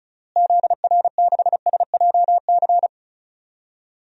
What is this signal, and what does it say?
Morse code, heard as a keyed tone, about 36 wpm, 700 Hz. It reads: ZR6SJC
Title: DXCC-100 Call Signs - 6 Characters - Repeated 3x - 35wpm